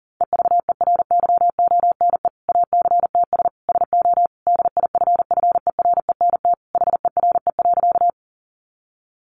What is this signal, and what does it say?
Morse code, heard as a keyed tone, about 40 wpm, 700 Hz. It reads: EVERYONE ACTS SO DIFFERENT HERE.
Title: Easy Phrases 40wpm